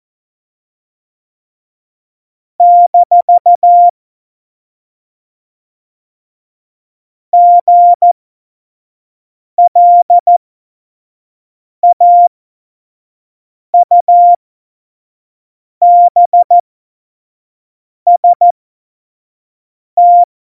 Morse code, heard as a keyed tone, about 14 wpm, spaced as for 5 wpm, 700 Hz. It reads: - GLAUBST
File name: Morsekurs / Witze / Witz 19. Januar 2021, 25 F70